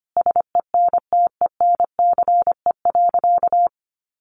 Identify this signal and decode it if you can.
Morse code, heard as a keyed tone, about 25 wpm, 700 Hz. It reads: SENTENCE.
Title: Additional Easy Phrases 25wpm